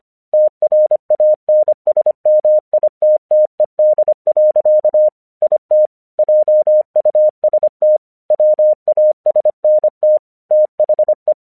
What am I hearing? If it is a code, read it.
Morse code, heard as a keyed tone, about 26 wpm, 610 Hz. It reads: TRANSMITTED. IT JUST WASNT THE
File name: 1 English 25wpm / AQC14